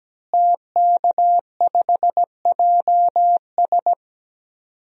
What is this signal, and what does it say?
Morse code, heard as a keyed tone, about 17 wpm, 700 Hz. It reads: TK5JS
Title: DXCC-100 Call Signs - 5 Characters - Repeated 3x - 17wpm